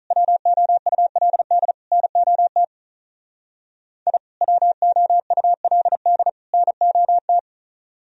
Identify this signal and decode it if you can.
Morse code, heard as a keyed tone, about 45 wpm, 700 Hz. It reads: WOULD NOT I WOULD NOT